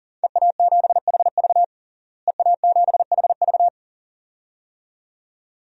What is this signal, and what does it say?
Morse code, heard as a keyed tone, about 40 wpm, 700 Hz. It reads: EA7HV EA7HV
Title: DXCC-100 Call Signs - 5 Characters - Repeated 2x - 40wpm